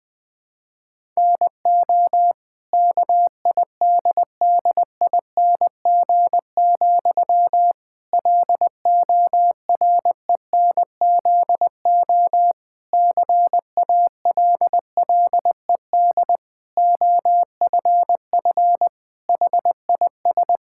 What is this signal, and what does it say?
Morse code, heard as a keyed tone, about 20 wpm, 700 Hz. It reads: NO KIDDING, LORENZO CALLED OFF HIS
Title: Pangram Challenge v2 - 20wpm